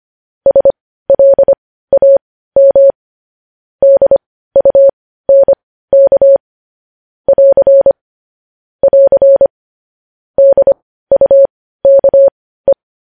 Morse code, heard as a keyed tone, about 25 wpm, spaced as for 15 wpm, 550 Hz. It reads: SLAM DUNK <AR> <AR> DUKE
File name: podcast / News headlines-1x-25x15